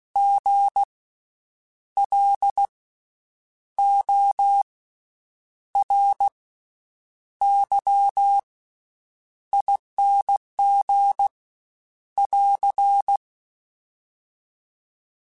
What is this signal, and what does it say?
Morse code, heard as a keyed tone, about 16 wpm, 800 Hz. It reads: G L O R Y ING <AR>
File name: Special INGs-mp3s / Special INGs-mp3s / Slow ing @15